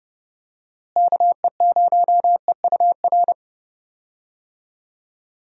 Morse code, heard as a keyed tone, about 30 wpm, 700 Hz. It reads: KE0EUR
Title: USA Call Signs - 6 Characters - Repeated 2x - 30wpm